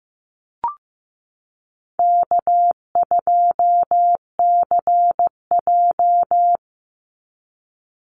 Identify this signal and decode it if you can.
Morse code, heard as a keyed tone, about 15 wpm, 700 Hz. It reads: K2CJ
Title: Call Signs Repeated 3x - 15wpm